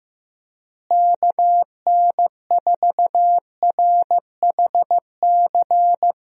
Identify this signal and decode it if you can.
Morse code, heard as a keyed tone, about 15 wpm, 700 Hz. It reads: KN4RHC
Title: USA Call Signs - 6 Characters - Repeated 3x - 15wpm